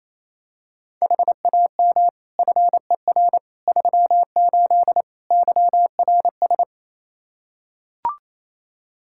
Morse code, heard as a keyed tone, about 28 wpm, 700 Hz. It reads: HAM FER 38 YRS
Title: K1USN SST QSO Practice 28wpm